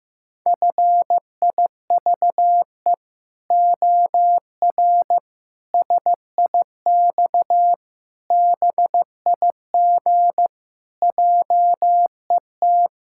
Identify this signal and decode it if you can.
Morse code, heard as a keyed tone, about 14 wpm, 700 Hz. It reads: FIVE OR SIX BIG JET
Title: Pangram Challenge v2 - 15wpm